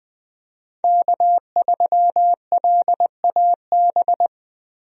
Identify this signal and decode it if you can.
Morse code, heard as a keyed tone, about 20 wpm, 700 Hz. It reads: K3LAB
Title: USA Call Signs - 5 Characters - 20wpm